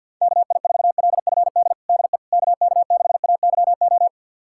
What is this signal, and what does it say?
Morse code, heard as a keyed tone, about 50 wpm, 700 Hz. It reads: KI4LFD DE KK6AYO
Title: USA Callsign DE Callsign 50wpm